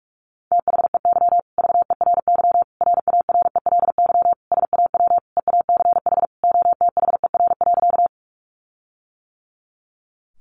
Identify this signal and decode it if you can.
Morse code, heard as a keyed tone, about 44 wpm, 700 Hz. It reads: THEY VERY RARELY SAW EACH OTHER.
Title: Sentences from Top 1400 Words 45wpm